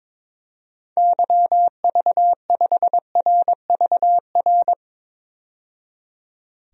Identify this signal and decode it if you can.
Morse code, heard as a keyed tone, about 22 wpm, 700 Hz. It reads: YV5RVR
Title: DXCC-100 Call Signs - 6 Characters - 22wpm